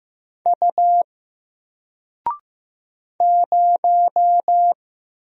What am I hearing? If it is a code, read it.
Morse code, heard as a keyed tone, about 15 wpm, 700 Hz. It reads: U 0
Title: Single Letter-Number Alt 15wpm